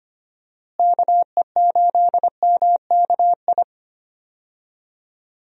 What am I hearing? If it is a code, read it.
Morse code, heard as a keyed tone, about 25 wpm, 700 Hz. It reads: KE8MKI
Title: USA Call Signs - 6 Characters - Repeated 3x - 25wpm